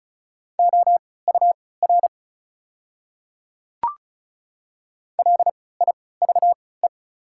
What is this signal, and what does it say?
Morse code, heard as a keyed tone, about 34 wpm, spaced as for 20 wpm, 700 Hz. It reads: OUR LIVE
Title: 3x Character Spacing - Top 200 Words - 35wpm